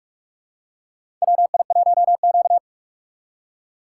Morse code, heard as a keyed tone, about 46 wpm, 700 Hz. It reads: WI1Q
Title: USA Call Signs - 4 Characters - 8x Character Spacing - 45wpm